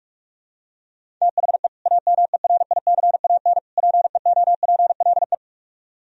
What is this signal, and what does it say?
Morse code, heard as a keyed tone, about 44 wpm, 700 Hz. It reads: THE AMERICAN PEOPLE